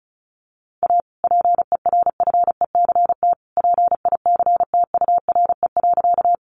Morse code, heard as a keyed tone, about 34 wpm, 700 Hz. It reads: A PERFECT PICTURE.
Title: Sentences from Top 1000 Words 35wpm